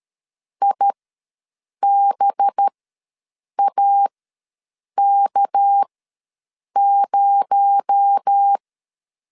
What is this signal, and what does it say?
Morse code, heard as a keyed tone, about 13 wpm, spaced as for 7 wpm, 800 Hz. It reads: IBAK0
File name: LEZIONI / 6 PKX0 / 6 PKX0 ws 12wpm 3448